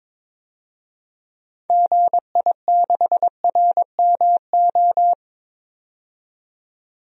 Morse code, heard as a keyed tone, about 22 wpm, 700 Hz. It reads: GI6RMO
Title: DXCC-100 Call Signs - 6 Characters - 22wpm